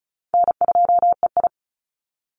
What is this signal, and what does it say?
Morse code, heard as a keyed tone, about 34 wpm, 700 Hz. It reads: N2EI
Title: Easy Call Signs Repeated 2x - 35wpm